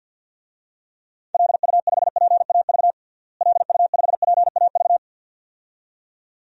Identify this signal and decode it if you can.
Morse code, heard as a keyed tone, about 51 wpm, 700 Hz. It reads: LU5PAV LU5PAV
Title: DXCC-100 Call Signs - 6 Characters - Repeated 2x - 50wpm